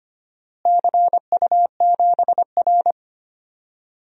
Morse code, heard as a keyed tone, about 25 wpm, 700 Hz. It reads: CU7R
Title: DXCC-100 Call Signs - 4 Characters - Repeated 2x - 25wpm